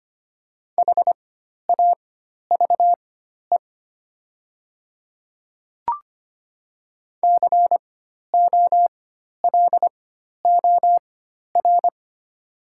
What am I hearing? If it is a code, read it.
Morse code, heard as a keyed tone, about 25 wpm, spaced as for 12 wpm, 700 Hz. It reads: HAVE COLOR
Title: 4x Character Spacing - Top 300 Words - 25wpm